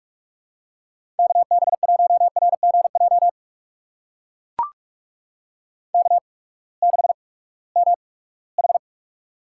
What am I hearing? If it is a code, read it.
Morse code, heard as a keyed tone, about 44 wpm, spaced as for 22 wpm, 700 Hz. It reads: KB1RGJ K6MH
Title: USA Call Signs - 8x Character Spacing - 45wpm